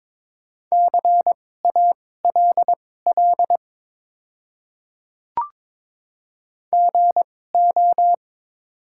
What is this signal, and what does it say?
Morse code, heard as a keyed tone, about 22 wpm, spaced as for 16 wpm, 700 Hz. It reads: CALL GO
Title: 2x Character Spacing - Top 100 Words - 22wpm